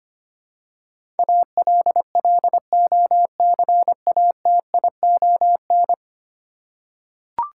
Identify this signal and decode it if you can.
Morse code, heard as a keyed tone, about 25 wpm, 700 Hz. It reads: ALLOCATION
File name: Ten Letter Words - 25wpm